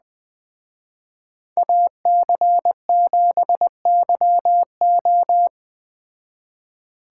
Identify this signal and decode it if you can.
Morse code, heard as a keyed tone, about 20 wpm, 700 Hz. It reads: AC7YO
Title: Speed Racing - USA 2x2 Call Signs 17wpm